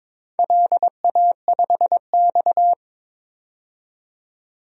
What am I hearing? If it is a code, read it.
Morse code, heard as a keyed tone, about 21 wpm, 700 Hz. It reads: LA5X
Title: DXCC-100 Call Signs - 4 Characters - 22wpm